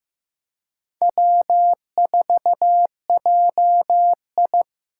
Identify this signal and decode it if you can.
Morse code, heard as a keyed tone, about 15 wpm, 700 Hz. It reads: W4JI
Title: USA Call Signs - 4 Characters - 4x Character Spacing - 15wpm